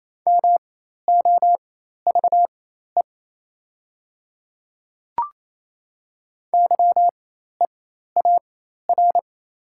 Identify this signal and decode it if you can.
Morse code, heard as a keyed tone, about 27 wpm, spaced as for 13 wpm, 690 Hz. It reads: MOVE YEAR
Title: 4x Character Spacing - Top 200 Words - 28wpm